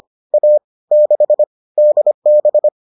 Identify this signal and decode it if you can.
Morse code, heard as a keyed tone, about 25 wpm, 600 Hz. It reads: A 6 DB